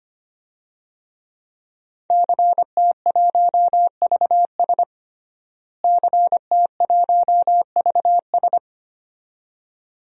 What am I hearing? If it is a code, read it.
Morse code, heard as a keyed tone, about 25 wpm, 700 Hz. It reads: CT1VS CT1VS
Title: DXCC-100 Call Signs - 5 Characters - Repeated 2x - 25wpm